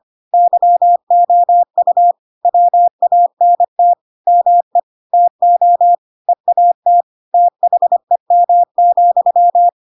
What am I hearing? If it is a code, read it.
Morse code, heard as a keyed tone, about 25 wpm, 700 Hz. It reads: YOU WANT ME TO EAT THEM,
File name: Pinocchio / Pinocchio ch7 25 25 ik1hge